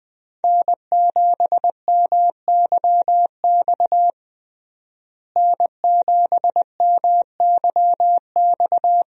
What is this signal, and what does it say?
Morse code, heard as a keyed tone, about 20 wpm, 700 Hz. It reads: N7MYX N7MYX
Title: USA Call Signs - 5 Characters - Repeated 2x - 20wpm